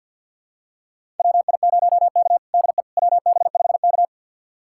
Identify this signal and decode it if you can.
Morse code, heard as a keyed tone, about 51 wpm, 700 Hz. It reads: WI0K DE WB5K
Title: USA Callsign DE Callsign - 4 Characters - 50wpm